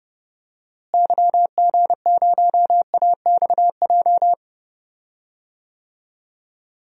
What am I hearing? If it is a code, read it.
Morse code, heard as a keyed tone, about 30 wpm, 700 Hz. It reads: YG0AXJ
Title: DXCC-100 Call Signs - 6 Characters - Repeated 3x - 30wpm